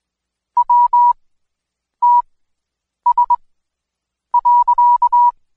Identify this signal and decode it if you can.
Morse code, heard as a keyed tone, about 20 wpm, spaced as for 8 wpm, 970 Hz. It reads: WTS.
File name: Code Quick / AudioDownld2 / Seg114-Day19-DayDrill 19